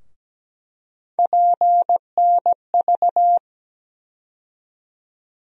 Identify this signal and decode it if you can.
Morse code, heard as a keyed tone, about 17 wpm, 700 Hz. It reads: PNV
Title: USA Call Sign Postfixes - 3 Letters 17wpm